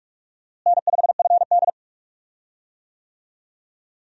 Morse code, heard as a keyed tone, about 45 wpm, 700 Hz. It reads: N5FD
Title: USA Call Signs - 4 Characters - Repeated 2x - 45wpm